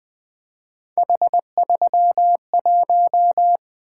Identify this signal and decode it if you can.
Morse code, heard as a keyed tone, about 20 wpm, 700 Hz. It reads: H31
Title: Three Letter-Number Combinations - 20wpm